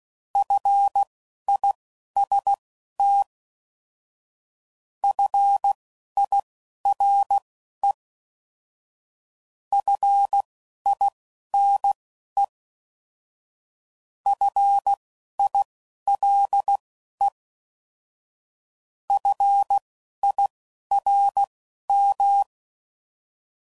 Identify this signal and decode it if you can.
Morse code, heard as a keyed tone, about 16 wpm, spaced as for 11 wpm, 800 Hz. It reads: FIST FIRE FINE FILE FIRM